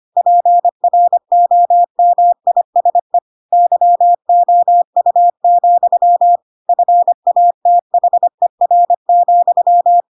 Morse code, heard as a keyed tone, about 25 wpm, 700 Hz. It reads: PROMISE YOU, FATHER,